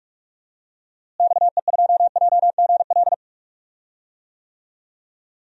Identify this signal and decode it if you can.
Morse code, heard as a keyed tone, about 45 wpm, 700 Hz. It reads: XE2JGL
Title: DXCC-100 Call Signs - 6 Characters - Repeated 3x - 45wpm